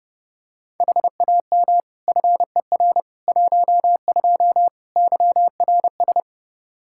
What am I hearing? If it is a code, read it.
Morse code, heard as a keyed tone, about 32 wpm, 700 Hz. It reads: HAM FER 12 YRS
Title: Practice QSO - Exchange 3 - More Info - 30wpm